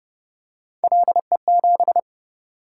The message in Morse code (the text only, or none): LE7